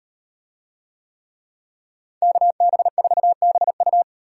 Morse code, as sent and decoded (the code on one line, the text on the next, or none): -.- -... ....- -... ..-
KB4BU